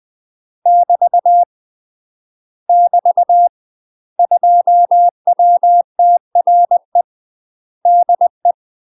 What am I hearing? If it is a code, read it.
Morse code, heard as a keyed tone, about 20 wpm, 700 Hz. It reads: <BT> <BT> 2WTRE DE
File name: UK MASTER QSOs TRAINING / 20WPM / UK MASTER QSOs 0010